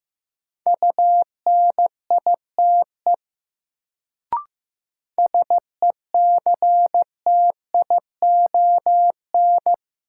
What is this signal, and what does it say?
Morse code, heard as a keyed tone, about 15 wpm, 700 Hz. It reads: UNITE SECTION
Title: Sentences from Top 1000 Words - Review Part 5 - New Word Drill - 15wpm